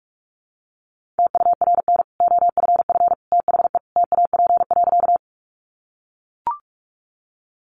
Morse code, heard as a keyed tone, about 44 wpm, 700 Hz. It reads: TURN OFF THE TAP.
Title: Sentences from Top 1400 Words 45wpm